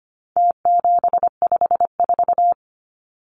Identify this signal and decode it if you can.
Morse code, heard as a keyed tone, about 25 wpm, 700 Hz. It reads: T754
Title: Numbers - Contest Style - 4-Digit 25wpm